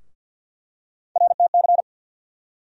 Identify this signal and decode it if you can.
Morse code, heard as a keyed tone, about 52 wpm, 690 Hz. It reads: RTC
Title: USA Call Sign Postfixes - 3 Letters 50wpm